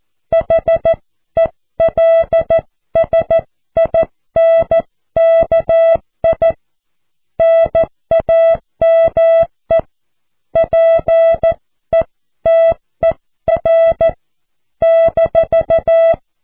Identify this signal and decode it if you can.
Morse code, heard as a keyed tone, about 14 wpm, 660 Hz. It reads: HELSINKI NAME PETER -
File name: cw / QSO